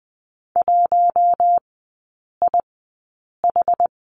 Morse code, heard as a keyed tone, about 20 wpm, spaced as for 8 wpm, 700 Hz. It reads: 1IH
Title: 060 Lesson 11e - Letter h - Three Characters 20wpm